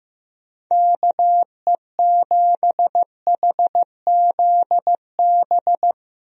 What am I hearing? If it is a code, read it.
Morse code, heard as a keyed tone, about 15 wpm, 700 Hz. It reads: KE7HZB
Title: USA Call Signs - 6 Characters - Repeated 2x - 15wpm